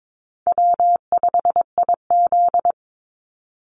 Morse code, heard as a keyed tone, about 22 wpm, 700 Hz. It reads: W5IZ
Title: Call Signs Repeated 3x - 22wpm